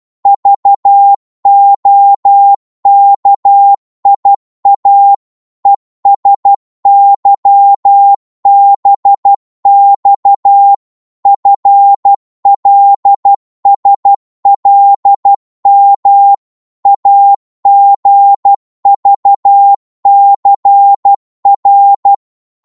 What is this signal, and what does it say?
Morse code, heard as a keyed tone, about 12 wpm, 800 Hz. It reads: VOKIA ESYBX FLSLM AGVCR
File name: Morse / ltr-12wpm